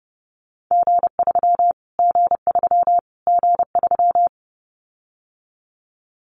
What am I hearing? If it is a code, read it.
Morse code, heard as a keyed tone, about 30 wpm, 700 Hz. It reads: G3 G3 G3
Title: Callsign Prefixes for Top 25 DXCC - Repeated 3x - 30wpm